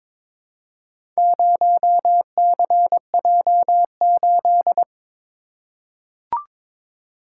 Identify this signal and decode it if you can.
Morse code, heard as a keyed tone, about 22 wpm, 700 Hz. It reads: 0CJ8
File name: practice / 22wpm / Four Letter-Number Combinations - 22wpm